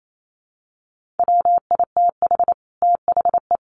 Morse code, heard as a keyed tone, about 28 wpm, 700 Hz. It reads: WITH THE
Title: Sets of 4 Words 28wpm